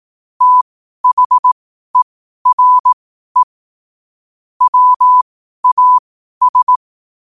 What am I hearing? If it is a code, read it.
Morse code, heard as a keyed tone, about 18 wpm, spaced as for 12 wpm, 1000 Hz. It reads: THERE WAS
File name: qotdfiles / 12wpm